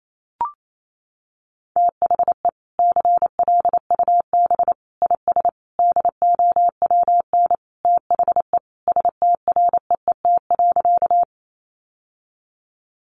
Morse code, heard as a keyed tone, about 28 wpm, 700 Hz. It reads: THE CLUB IS DOWN THE STREET.
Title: Sentences from Top 1200 Words 28wpm